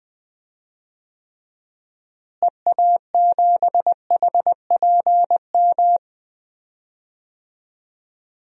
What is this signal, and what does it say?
Morse code, heard as a keyed tone, about 20 wpm, 700 Hz. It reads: EA7HPM
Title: DXCC-100 Call Signs - 6 Characters - Repeated 3x - 20wpm